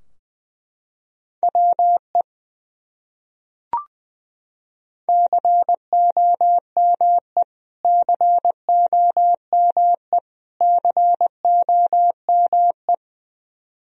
Morse code, heard as a keyed tone, about 20 wpm, 700 Hz. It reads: WE COME COME COME
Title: Top 100 Words Repeated 20wpm